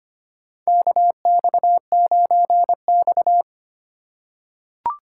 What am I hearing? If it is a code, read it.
Morse code, heard as a keyed tone, about 25 wpm, 700 Hz. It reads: KX9X
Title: USA Call Signs - 4x Character Spacing - 25wpm